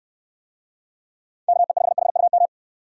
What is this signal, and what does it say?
Morse code, heard as a keyed tone, about 63 wpm, 700 Hz. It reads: KE5HUM